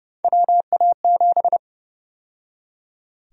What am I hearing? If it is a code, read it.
Morse code, heard as a keyed tone, about 30 wpm, 700 Hz. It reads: WA7
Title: USA Call Sign Prefixes - 30wpm